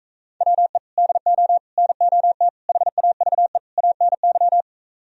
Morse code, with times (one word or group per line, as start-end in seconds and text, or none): 0.4-0.8 WE
1.0-1.6 DO
1.8-2.5 NOT
2.7-3.6 HAVE
3.8-4.6 ANY